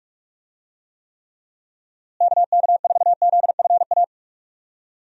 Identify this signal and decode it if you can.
Morse code, heard as a keyed tone, about 46 wpm, 700 Hz. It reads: KK4ZFA
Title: USA Call Signs - 6 Characters - Repeated 3x - 45wpm